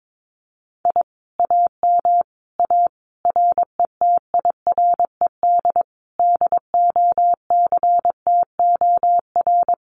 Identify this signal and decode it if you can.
Morse code, heard as a keyed tone, about 22 wpm, 700 Hz. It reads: I AM A RETIRED DOCTOR
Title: Common QSO Phrases Encore 22wpm